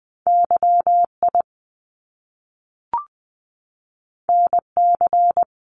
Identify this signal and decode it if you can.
Morse code, heard as a keyed tone, about 20 wpm, 700 Hz. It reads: YI NC
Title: Top Two Letter Combinations in Words 20wpm